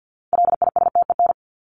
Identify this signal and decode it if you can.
Morse code, heard as a keyed tone, about 50 wpm, 700 Hz. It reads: LISTEN
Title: Top 500 Words 50wpm